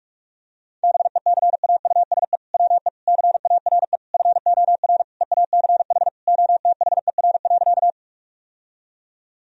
Morse code, heard as a keyed tone, about 46 wpm, 700 Hz. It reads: BECAUSE WE CARE FOR EACH OTHER.
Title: Sentences from Top 1000 Words - Review Part 3 - 45wpm